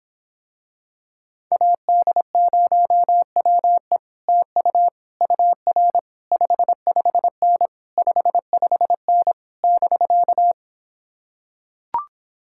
Morse code, heard as a keyed tone, about 26 wpm, 700 Hz. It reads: AD0WE TU UR 55N 55N <BK>